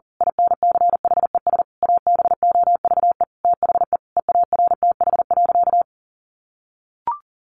Morse code, heard as a keyed tone, about 37 wpm, 700 Hz. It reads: INCHES ABOVE THE EARTH.
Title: A Sound Of Thunder 40wpm